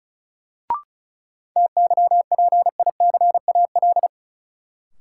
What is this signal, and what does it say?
Morse code, heard as a keyed tone, about 34 wpm, 700 Hz. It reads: TYPICAL
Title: Sentences from Top 1100 Words - New Word Drill - 35wpm